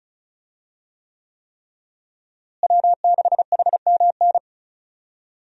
Morse code, heard as a keyed tone, about 34 wpm, 700 Hz. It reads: W6HMN
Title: USA Call Signs - 5 Characters - Repeated 3x - 35wpm